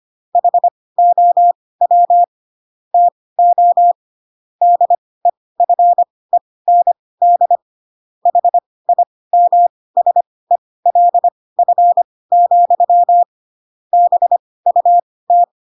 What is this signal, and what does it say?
Morse code, heard as a keyed tone, about 25 wpm, spaced as for 18 wpm, 700 Hz. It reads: HOW TO DEFEND HIMSELF, BUT
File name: Pinocchio ch3 12 25 ik1hge